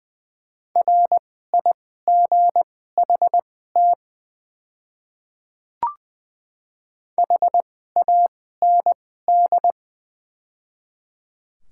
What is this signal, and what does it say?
Morse code, heard as a keyed tone, about 20 wpm, spaced as for 14 wpm, 700 Hz. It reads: RIGHT HAND
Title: 2x Character Spacing - Top 300 Words - 20wpm